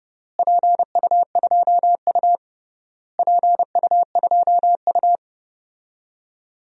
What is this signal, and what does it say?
Morse code, heard as a keyed tone, about 30 wpm, 700 Hz. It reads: PU2U PU2U
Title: DXCC-100 Call Signs - 4 Characters - Repeated 3x - 30wpm